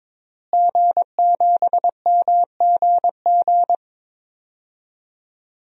Morse code, heard as a keyed tone, about 22 wpm, 700 Hz. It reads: G7MGG
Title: DXCC-100 Call Signs - 5 Characters - Repeated 2x - 22wpm